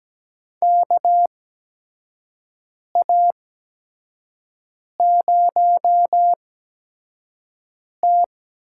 Morse code, heard as a keyed tone, about 17 wpm, spaced as for 5 wpm, 700 Hz. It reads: KA0T
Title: USA Call Signs - 4 Characters - 8x Character Spacing - 17wpm